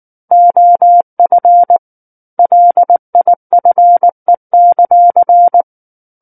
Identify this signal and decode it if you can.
Morse code, heard as a keyed tone, about 19 wpm, 700 Hz. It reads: OF LIFE;
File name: Book One / 10-Chapter 10 - 19 WPM